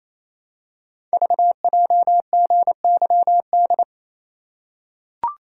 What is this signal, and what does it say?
Morse code, heard as a keyed tone, about 28 wpm, 700 Hz. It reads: VJGYD
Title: Code Groups - 5 Letters - 28wpm